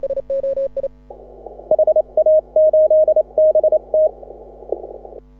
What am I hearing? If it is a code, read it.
Morse code, heard as a keyed tone, about 28 wpm, 620 Hz. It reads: HA8BT
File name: AUDIO2018 / HA6OI CW